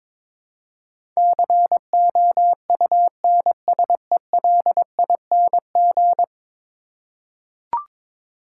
Morse code, heard as a keyed tone, about 22 wpm, 700 Hz. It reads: COUNSELING
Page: morse-code-ninja-practice-22wpm